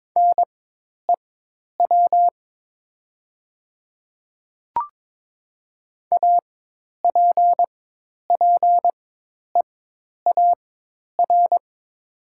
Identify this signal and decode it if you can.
Morse code, heard as a keyed tone, about 22 wpm, spaced as for 10 wpm, 700 Hz. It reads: NEW APPEAR
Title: 4x Character Spacing - Top 500 Words - 22wpm